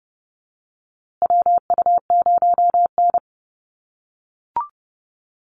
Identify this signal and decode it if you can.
Morse code, heard as a keyed tone, about 30 wpm, 700 Hz. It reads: WU0N
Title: USA Call Signs Repeated 2x 30wpm